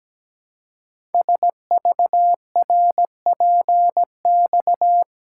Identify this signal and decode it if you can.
Morse code, heard as a keyed tone, about 17 wpm, 700 Hz. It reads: SVRPX
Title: Code Groups - 5 Letters - 17wpm